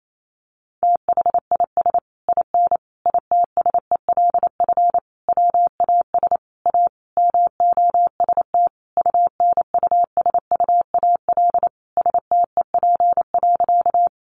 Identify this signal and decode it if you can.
Morse code, heard as a keyed tone, about 28 wpm, 700 Hz. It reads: THIS IN ITSELF WAS A MOST UNUSUAL STEP.
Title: Sentences from Top 1400 Words 28wpm